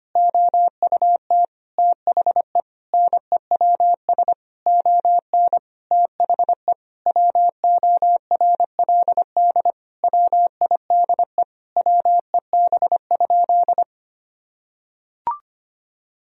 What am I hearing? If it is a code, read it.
Morse code, heard as a keyed tone, about 25 wpm, 700 Hz. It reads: OUT THE NEWS ON THE WORLD WIDE WEB?
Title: Sentences from Top 1000 Words - Review Part 4 - 25wpm